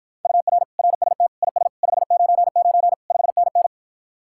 Morse code, heard as a keyed tone, about 52 wpm, 700 Hz. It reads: UR RST IS 599 5NN